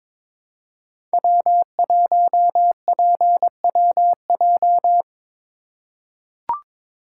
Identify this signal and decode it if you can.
Morse code, heard as a keyed tone, about 22 wpm, 700 Hz. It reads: W1PWJ
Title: USA Call Signs - 5 Characters - 3x Character Spacing - 22wpm